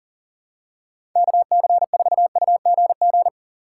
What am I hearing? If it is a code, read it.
Morse code, heard as a keyed tone, about 40 wpm, 700 Hz. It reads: KC4UGG
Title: USA Call Signs - 6 Characters - 2x Character Spacing - 40wpm